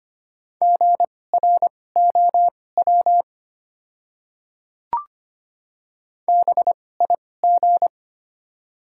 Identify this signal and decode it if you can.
Morse code, heard as a keyed tone, about 25 wpm, spaced as for 18 wpm, 700 Hz. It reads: GROW BIG
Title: 2x Character Spacing - Top 300 Words - 25wpm